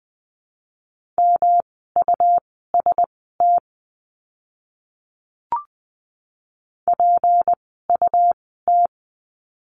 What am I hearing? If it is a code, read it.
Morse code, heard as a keyed tone, about 20 wpm, spaced as for 14 wpm, 700 Hz. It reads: MUST PUT
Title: 237 Lesson 38f - Letter q - Words 20wpm